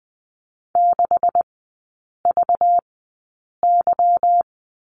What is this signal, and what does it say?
Morse code, heard as a keyed tone, about 20 wpm, spaced as for 8 wpm, 700 Hz. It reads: 6VY
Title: 218 Lesson 35e - Number 0 - Three Characters 20wpm